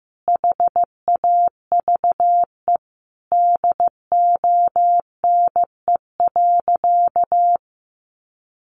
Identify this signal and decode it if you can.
Morse code, heard as a keyed tone, about 15 wpm, 700 Hz. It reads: HAVE DONE.
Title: Sentences from Top 500 Words 15wpm